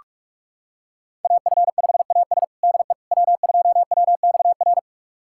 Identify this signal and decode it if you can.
Morse code, heard as a keyed tone, about 46 wpm, 700 Hz. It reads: AF5AS DE W2WXR
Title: USA Callsign DE Callsign - 5 Characters - 45wpm